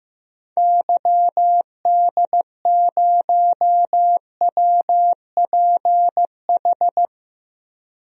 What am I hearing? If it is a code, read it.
Morse code, heard as a keyed tone, about 15 wpm, 700 Hz. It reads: YD0WPH
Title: DXCC-100 Call Signs - 6 Characters - Repeated 3x - 15wpm